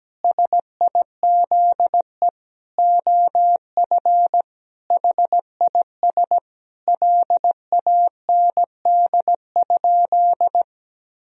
Morse code, heard as a keyed tone, about 17 wpm, 700 Hz. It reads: SIZE OF HIS LAND?
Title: Sentences from Top 1000 Words - Review Part 2 - 17wpm